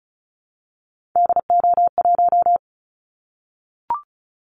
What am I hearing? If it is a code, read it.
Morse code, heard as a keyed tone, about 36 wpm, 700 Hz. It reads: DO1
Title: Callsign Prefixes for Top 25 DXCC - 35wpm